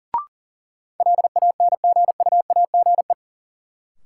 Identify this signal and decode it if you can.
Morse code, heard as a keyed tone, about 39 wpm, 700 Hz. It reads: LANGUAGE
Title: Sentences from Top 500 Words - New Word Drill - 40wpm